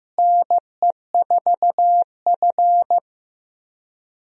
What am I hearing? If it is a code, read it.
Morse code, heard as a keyed tone, about 15 wpm, 700 Hz. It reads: NE4F
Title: USA Call Signs - 4 Characters - Repeated 3x - 15wpm